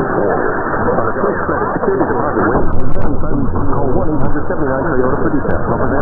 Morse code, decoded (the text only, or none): none